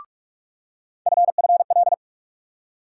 Morse code, heard as a keyed tone, about 42 wpm, 700 Hz. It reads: FFL
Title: USA Call Sign Postfixes - 3 Letters 45wpm